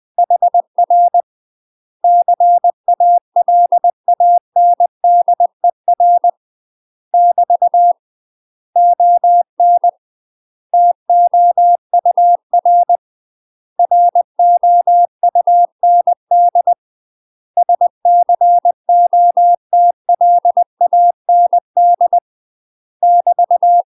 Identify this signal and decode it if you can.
Morse code, heard as a keyed tone, about 20 wpm, 700 Hz. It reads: HR CALANDER <BT> ON TOUR ROUND SCOTLAND <BT>
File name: UK MASTER QSOs TRAINING / 20WPM / UK MASTER QSOs 0024